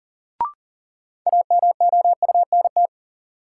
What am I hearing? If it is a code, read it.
Morse code, heard as a keyed tone, about 40 wpm, 700 Hz. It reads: AMOUNT
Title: Sets of 3 Words - New Word Drill - 40wpm